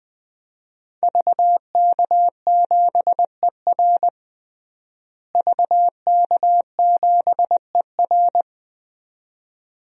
Morse code, heard as a keyed tone, about 21 wpm, 700 Hz. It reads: VK7ER VK7ER
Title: DXCC-100 Call Signs - 5 Characters - Repeated 2x - 20wpm